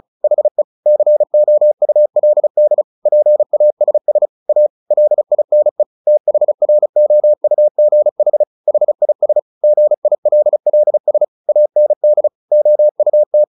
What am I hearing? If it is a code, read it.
Morse code, heard as a keyed tone, about 34 wpm, 600 Hz. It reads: HE COULD PASS A LINE THROUGH HIS GILLS AND OUT
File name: the old man and the sea 2 35wpm